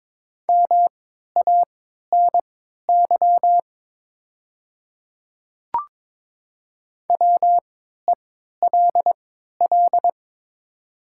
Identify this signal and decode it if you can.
Morse code, heard as a keyed tone, about 22 wpm, spaced as for 12 wpm, 700 Hz. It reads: MANY WELL
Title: 3x Character Spacing - Top 300 Words - 22wpm